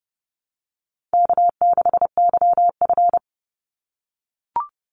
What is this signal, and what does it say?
Morse code, heard as a keyed tone, about 30 wpm, 700 Hz. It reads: K6YF
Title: Easy Call Signs - 30wpm